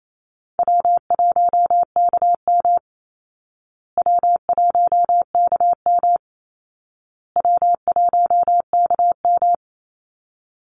Morse code, heard as a keyed tone, about 28 wpm, 700 Hz. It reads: W1KM W1KM W1KM
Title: Easy Call Signs Repeated 3x - 28wpm